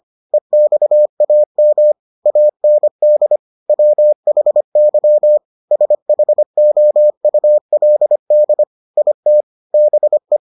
Morse code, heard as a keyed tone, about 25 wpm, 600 Hz. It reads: EXAM AND WHY SHOULD IT BE